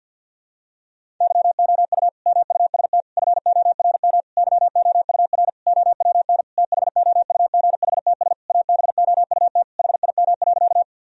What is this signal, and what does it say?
Morse code, heard as a keyed tone, about 52 wpm, 700 Hz. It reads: YOU MUST FORM YOUR OWN THOUGHTS ABOUT HIM.
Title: Sentences from Top 1000 Words - Review Part 1 - 50wpm